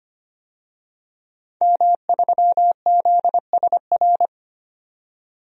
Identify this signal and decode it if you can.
Morse code, heard as a keyed tone, about 25 wpm, 700 Hz. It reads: M3ZSR
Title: DXCC-100 Call Signs - 5 Characters - 25wpm